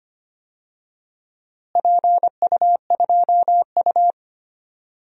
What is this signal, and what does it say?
Morse code, heard as a keyed tone, about 25 wpm, 700 Hz. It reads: PU2U
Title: DXCC-100 Call Signs - 4 Characters - Repeated 3x - 25wpm